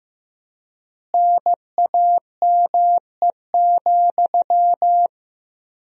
Morse code, heard as a keyed tone, about 15 wpm, 700 Hz. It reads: NAME,
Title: 4x Word-Spacing - Sentences from Top 500 Words - 15wpm